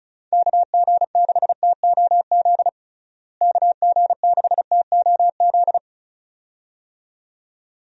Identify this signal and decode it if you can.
Morse code, heard as a keyed tone, about 36 wpm, 700 Hz. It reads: KG6TOZ KG6TOZ
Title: USA Call Signs - 6 Characters - Repeated 2x - 35wpm